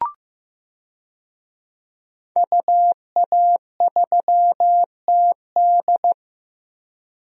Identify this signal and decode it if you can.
Morse code, heard as a keyed tone, about 15 wpm, 700 Hz. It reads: UA3TD
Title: DXCC-100 Call Signs - 5 Characters - Repeated 3x - 15wpm